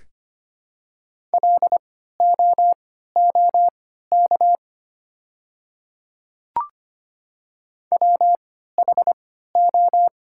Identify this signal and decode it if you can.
Morse code, heard as a keyed tone, about 25 wpm, spaced as for 14 wpm, 700 Hz. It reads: LOOK WHO
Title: 3x Character Spacing - Top 100 Words - 25wpm